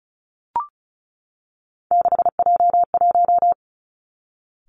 Morse code, heard as a keyed tone, about 36 wpm, 700 Hz. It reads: 6J1
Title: Three Challenging Letter-Number Combinations 35wpm